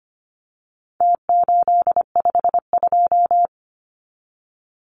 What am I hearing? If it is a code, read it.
Morse code, heard as a keyed tone, about 25 wpm, 700 Hz. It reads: T852
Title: Numbers - Contest Style - 4-Digit 25wpm